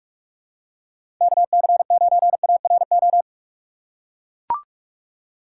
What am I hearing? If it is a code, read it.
Morse code, heard as a keyed tone, about 44 wpm, 700 Hz. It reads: KC9ARO
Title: USA Call Signs - 6 Characters - 3x Character Spacing - 45wpm